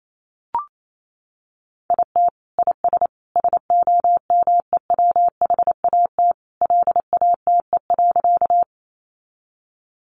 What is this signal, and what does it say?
Morse code, heard as a keyed tone, about 28 wpm, 700 Hz. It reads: IT IS SOMEWHAT LATE.
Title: Sentences from Top 1300 Words 28wpm